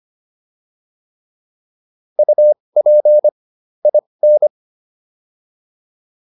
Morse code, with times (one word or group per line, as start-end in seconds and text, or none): 2.2-3.3 UP
3.9-4.5 IN